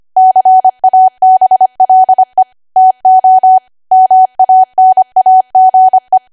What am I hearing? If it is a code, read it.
Morse code, heard as a keyed tone, about 25 wpm, 750 Hz. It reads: CABLE TO MANAGE
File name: Archive / 25 WPM / 25WPM